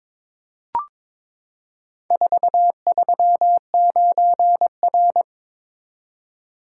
Morse code, heard as a keyed tone, about 22 wpm, 700 Hz. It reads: 439R
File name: practice / 22wpm / Four Letter-Number Combinations - 22wpm